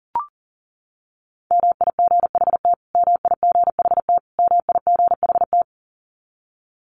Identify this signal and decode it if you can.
Morse code, heard as a keyed tone, about 40 wpm, 700 Hz. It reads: MIGHT MIGHT MIGHT
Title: Top 300 Words Repeated 40wpm